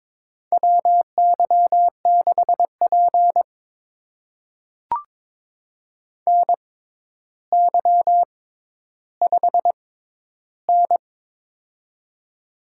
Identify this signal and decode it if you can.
Morse code, heard as a keyed tone, about 22 wpm, spaced as for 11 wpm, 700 Hz. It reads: WY6P NY5N
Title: USA Call Signs - 4 Characters - 6x Character Spacing - 22wpm